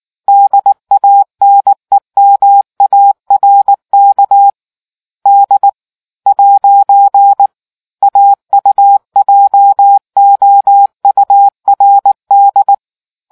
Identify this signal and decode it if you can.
Morse code, heard as a keyed tone, about 19 wpm, 800 Hz. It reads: DANEMARK D ' AUJOURD